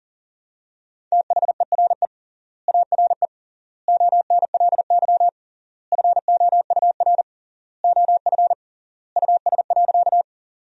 Morse code, heard as a keyed tone, about 40 wpm, 700 Hz. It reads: THERE ARE ONLY FOUR OF US.
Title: 3x Word-Spacing - Sentences from Top 300 Words - 40wpm